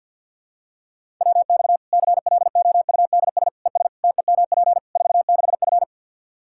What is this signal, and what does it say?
Morse code, heard as a keyed tone, about 52 wpm, 700 Hz. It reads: WX CLOUDS ES TEMP 46F
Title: Practice QSO - Exchange 2 - Info - 50wpm